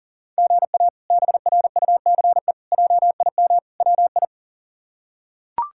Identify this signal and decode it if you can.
Morse code, heard as a keyed tone, about 40 wpm, 700 Hz. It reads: GA BRUCE JIM WI
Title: K1USN SST QSO Practice - 40wpm